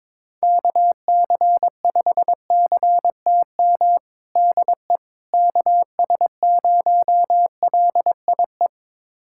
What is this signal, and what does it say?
Morse code, heard as a keyed tone, about 21 wpm, 700 Hz. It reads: KC5CTM DE KS0LIE
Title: USA Callsign DE Callsign - 6 Characters - 22wpm